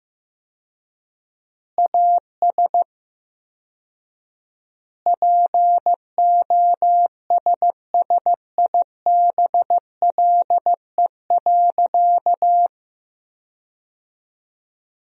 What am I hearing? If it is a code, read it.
Morse code, heard as a keyed tone, about 15 wpm, 700 Hz. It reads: AS POSSIBLE.
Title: 4x Word-Spacing - Sentences from Top 500 Words - 15wpm